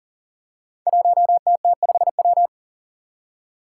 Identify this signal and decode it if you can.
Morse code, heard as a keyed tone, about 41 wpm, 700 Hz. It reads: 1TT5W
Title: Five Challenging Letter-Number Combinations 40wpm